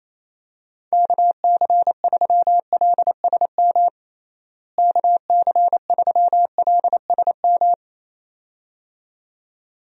Morse code, heard as a keyed tone, about 28 wpm, 700 Hz. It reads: KC3LSM KC3LSM
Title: USA Call Signs - 6 Characters - Repeated 2x - 28wpm